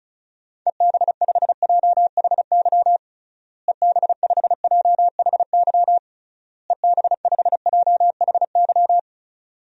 Morse code, heard as a keyed tone, about 36 wpm, 700 Hz. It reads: EB5JHY EB5JHY EB5JHY
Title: DXCC-100 Call Signs - 6 Characters - Repeated 3x - 35wpm